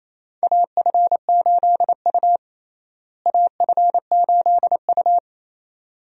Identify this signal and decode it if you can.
Morse code, heard as a keyed tone, about 27 wpm, 700 Hz. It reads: AF8U AF8U
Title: USA Call Signs - 4 Characters - Repeated 2x - 28wpm